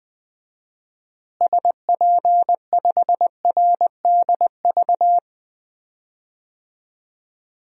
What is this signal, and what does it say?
Morse code, heard as a keyed tone, about 19 wpm, 700 Hz. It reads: SP5RDV
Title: DXCC-100 Call Signs - 6 Characters - Repeated 3x - 20wpm